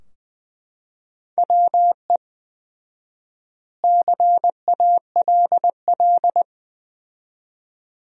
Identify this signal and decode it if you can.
Morse code, heard as a keyed tone, about 20 wpm, 700 Hz. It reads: WE CALL
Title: 4x Word-Spacing - Sentences from Top 400 Words - 20wpm